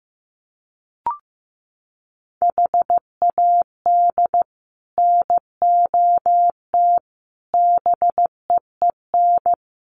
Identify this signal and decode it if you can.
Morse code, heard as a keyed tone, about 15 wpm, 700 Hz. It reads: HAD NOT BEEN
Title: Sets of 5 Words Encore 15wpm